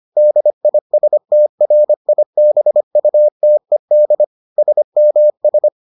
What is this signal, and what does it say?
Morse code, heard as a keyed tone, about 25 wpm, 600 Hz. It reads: DISTRIBUTED SMS